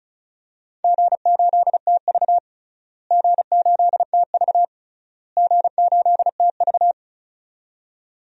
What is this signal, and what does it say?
Morse code, heard as a keyed tone, about 36 wpm, 700 Hz. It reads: G8TV G8TV G8TV
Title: DXCC-100 Call Signs - 4 Characters - Repeated 3x - 35wpm